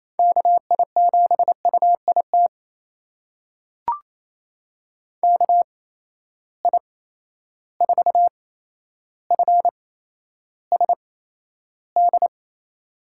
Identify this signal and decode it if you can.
Morse code, heard as a keyed tone, about 28 wpm, spaced as for 12 wpm, 700 Hz. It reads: KI7UIT KI4FSD
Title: USA Call Signs - 6 Characters - 8x Character Spacing - 28wpm